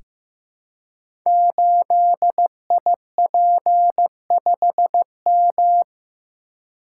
Morse code, heard as a keyed tone, about 15 wpm, 700 Hz. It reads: 8IP5M
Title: Code Groups - 5 Letter:Numbers - 15wpm